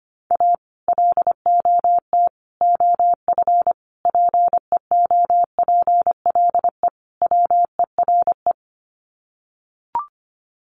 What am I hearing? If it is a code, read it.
Morse code, heard as a keyed tone, about 25 wpm, 700 Hz. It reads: A LOT OF PEOPLE WERE
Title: Sets of 5 Words Encore2 25wpm